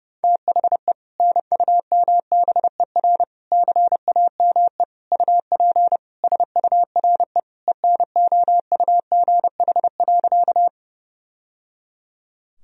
Morse code, heard as a keyed tone, about 30 wpm, 700 Hz. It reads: THE NUMBER CAME UP SURE ENOUGH.
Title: Sentences from Top 1000 Words - Review Part 1 - 30wpm